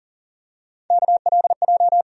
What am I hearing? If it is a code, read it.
Morse code, heard as a keyed tone, about 40 wpm, 700 Hz. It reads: KLJ
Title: USA Call Sign Postfixes - 3 Letters 40wpm